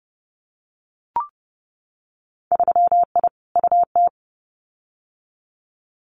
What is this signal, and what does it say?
Morse code, heard as a keyed tone, about 31 wpm, 700 Hz. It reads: 3I UT
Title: Winter Field Day Exchanges Encore 30wpm